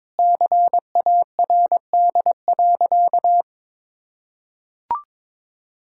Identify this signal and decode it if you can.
Morse code, heard as a keyed tone, about 22 wpm, 700 Hz. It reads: CARD.
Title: Sentences from Top 1000 Words - Review Part 6 - 22wpm